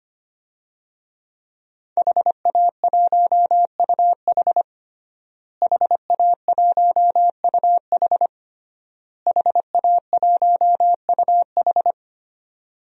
Morse code, heard as a keyed tone, about 25 wpm, 700 Hz. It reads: HA1UH HA1UH HA1UH
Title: DXCC-100 Call Signs - 5 Characters - Repeated 3x - 25wpm